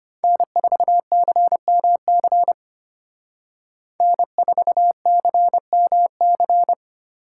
Morse code, heard as a keyed tone, about 30 wpm, 700 Hz. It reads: N4CMC N4CMC
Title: Speed Racing - USA 1x3 Call Signs 25wpm